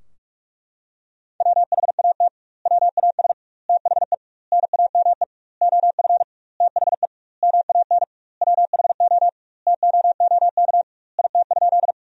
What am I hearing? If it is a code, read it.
Morse code, heard as a keyed tone, about 41 wpm, 700 Hz. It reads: WHAT WAS THE NAME OF THE MAN WHO TOOK IT?
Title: 2x Word-Spacing - Sentences from Top 300 Words - 45wpm